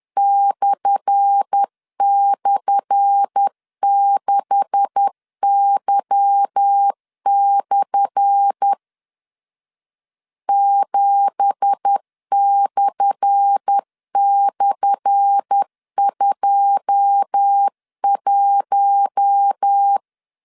Morse code, heard as a keyed tone, about 10 wpm, 800 Hz. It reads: //6Y/ 7//21
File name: LEZIONI / 10 SIGNS / 10A SLASH 10wpm 2808